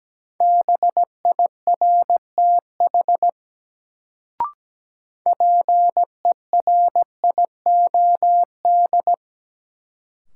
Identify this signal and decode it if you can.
Morse code, heard as a keyed tone, about 17 wpm, 700 Hz. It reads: BIRTH PERIOD
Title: Sentences from Top 800 Words - New Word Drill - 17wpm